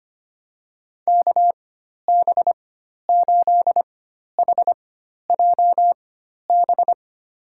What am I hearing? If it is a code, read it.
Morse code, heard as a keyed tone, about 25 wpm, spaced as for 12 wpm, 700 Hz. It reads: KB8HJB
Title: USA Call Signs - 6 Characters - 4x Character Spacing - 25wpm